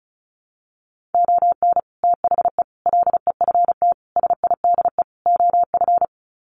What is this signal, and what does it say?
Morse code, heard as a keyed tone, about 34 wpm, 700 Hz. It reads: ON THE LEFT SIDE OF
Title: Sets of 5 Words Encore2 35wpm